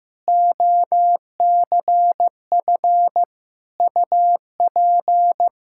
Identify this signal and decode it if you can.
Morse code, heard as a keyed tone, about 16 wpm, 700 Hz. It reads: OCF UP
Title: Practice QSO - Exchange 2 - Info - 15wpm